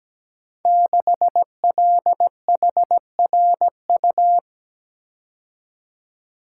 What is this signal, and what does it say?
Morse code, heard as a keyed tone, about 17 wpm, 700 Hz. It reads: 6LHRU
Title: Code Groups - 5 Letter:Numbers - 17wpm